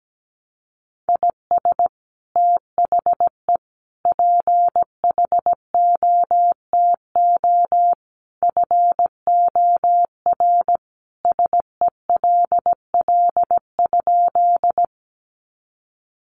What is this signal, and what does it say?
Morse code, heard as a keyed tone, about 17 wpm, 700 Hz. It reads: IS THE PHOTO FOR SELL?
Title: Sentences from Top 1200 Words 17wpm